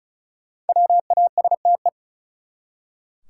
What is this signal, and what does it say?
Morse code, heard as a keyed tone, about 34 wpm, 700 Hz. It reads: WASTE
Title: Five Letter Words - 35wpm